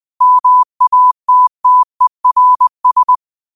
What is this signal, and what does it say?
Morse code, heard as a keyed tone, about 20 wpm, 1000 Hz. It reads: MATTERS